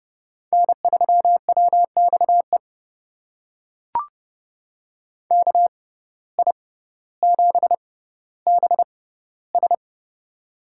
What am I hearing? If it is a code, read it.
Morse code, heard as a keyed tone, about 30 wpm, spaced as for 15 wpm, 700 Hz. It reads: N3WXE KI7BS
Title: USA Call Signs - 5 Characters - 6x Character Spacing - 30wpm